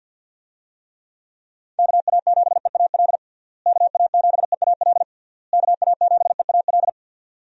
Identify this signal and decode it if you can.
Morse code, heard as a keyed tone, about 49 wpm, 700 Hz. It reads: KA7EAL KA7EAL KA7EAL
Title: USA Call Signs - 6 Characters - Repeated 3x - 50wpm